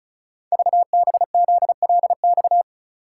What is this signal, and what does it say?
Morse code, heard as a keyed tone, about 36 wpm, 700 Hz. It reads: VBZLX
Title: Code Groups - 5 Letters - 35wpm